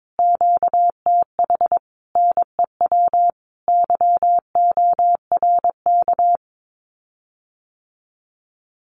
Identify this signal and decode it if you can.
Morse code, heard as a keyed tone, about 22 wpm, 700 Hz. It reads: QTH NEW YORK
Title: Common QSO Phrases Encore 22wpm